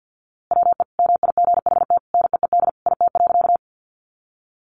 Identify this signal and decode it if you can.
Morse code, heard as a keyed tone, about 50 wpm, 700 Hz. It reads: WE MIGHT NEED IT.
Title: A Sound Of Thunder 50wpm